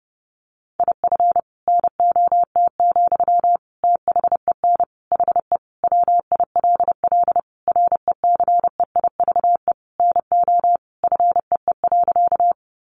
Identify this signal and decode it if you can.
Morse code, heard as a keyed tone, about 30 wpm, 700 Hz. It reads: IF NOT, THEN HE WILL RECEIVE NO FEE.
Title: Sentences from Top 1500 Words 30wpm